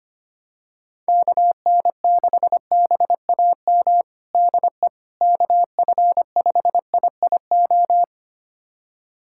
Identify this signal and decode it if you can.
Morse code, heard as a keyed tone, about 25 wpm, 700 Hz. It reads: KN6BAM DE KF5IIO
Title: USA Callsign DE Callsign - 6 Characters - 25wpm